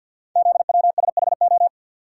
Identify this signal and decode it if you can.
Morse code, heard as a keyed tone, about 52 wpm, 700 Hz. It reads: ZWSHO
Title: Code Groups - 5 Letters - 50wpm